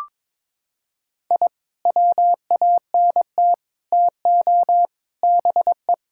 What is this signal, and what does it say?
Morse code, heard as a keyed tone, about 22 wpm, 700 Hz. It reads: I WANT TO BE
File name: practice / 22wpm / Sentences with 4-Word Phrases - 22wpm